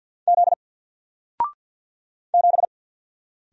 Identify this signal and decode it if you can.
Morse code, heard as a keyed tone, about 49 wpm, 700 Hz. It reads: Z 7
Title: Single Character (Letter:Number:Punctuation:Prosign) Alt 50wpm